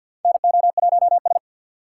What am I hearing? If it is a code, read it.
Morse code, heard as a keyed tone, about 46 wpm, 700 Hz. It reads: NO1S
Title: USA Call Signs - 4 Characters - 2x Character Spacing - 50wpm